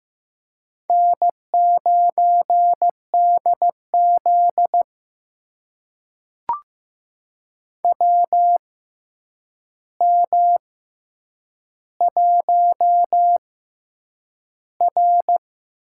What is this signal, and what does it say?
Morse code, heard as a keyed tone, about 15 wpm, spaced as for 8 wpm, 700 Hz. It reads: N9DZ WM1R
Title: USA Call Signs - 4 Characters - 6x Character Spacing - 15wpm